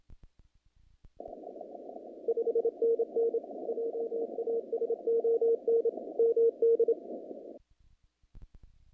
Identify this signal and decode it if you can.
Morse code, heard as a keyed tone, about 29 wpm, 460 Hz. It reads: 5NN JASON MD
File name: contests / naqp-cw-k5tr-op-k5pi- -2100